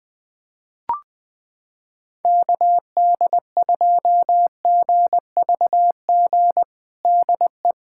Morse code, heard as a keyed tone, about 21 wpm, 700 Hz. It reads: KD2GVG DE
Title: USA Callsign DE Callsign - 6 Characters - 20wpm